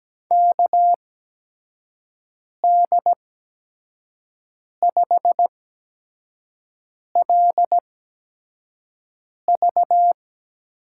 Morse code, heard as a keyed tone, about 17 wpm, spaced as for 5 wpm, 700 Hz. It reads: KD5LV
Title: USA Call Signs - 5 Characters - 8x Character Spacing - 17wpm